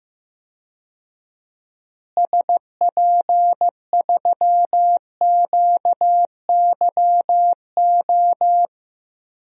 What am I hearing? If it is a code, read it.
Morse code, heard as a keyed tone, about 15 wpm, 700 Hz. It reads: SP3QYO
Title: DXCC-100 Call Signs - 6 Characters - Repeated 3x - 15wpm